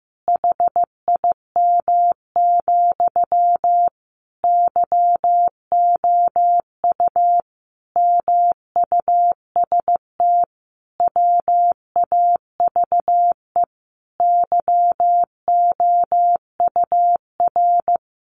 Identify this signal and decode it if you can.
Morse code, heard as a keyed tone, about 15 wpm, 700 Hz. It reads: HIM, YOU MUST WAVE YOUR
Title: Sentences from Top 1400 Words 15wpm